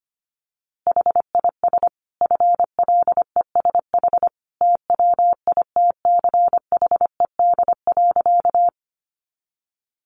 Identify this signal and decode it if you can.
Morse code, heard as a keyed tone, about 25 wpm, 700 Hz. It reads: HIS FLESH TWITCHED.
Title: A Sound Of Thunder 25wpm